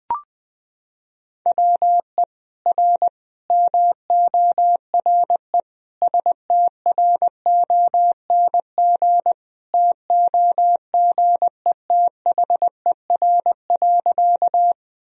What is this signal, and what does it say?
Morse code, heard as a keyed tone, about 20 wpm, 700 Hz. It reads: WE R MORE STRONG TOGETHER.
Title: Sentences from Top 500 Words and Abbreviations 20wpm